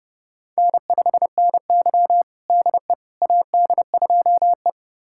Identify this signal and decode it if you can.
Morse code, heard as a keyed tone, about 30 wpm, 700 Hz. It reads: N5NY DE AD2E
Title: USA Callsign DE Callsign - 4 Characters - 30wpm